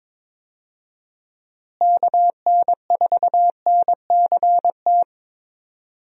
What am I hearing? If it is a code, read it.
Morse code, heard as a keyed tone, about 21 wpm, 700 Hz. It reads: KN4NCT
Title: USA Call Signs - 6 Characters - Repeated 2x - 22wpm